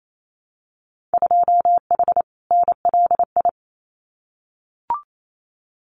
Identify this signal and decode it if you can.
Morse code, heard as a keyed tone, about 28 wpm, 700 Hz. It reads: 2H NLI
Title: Winter Field Day Exchanges 28wpm